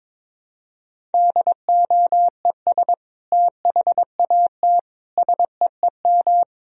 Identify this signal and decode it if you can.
Morse code, heard as a keyed tone, about 22 wpm, 700 Hz. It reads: DOES THAT SEEM
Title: Sentences from Top 1000 Words - Review Part 1 - 22wpm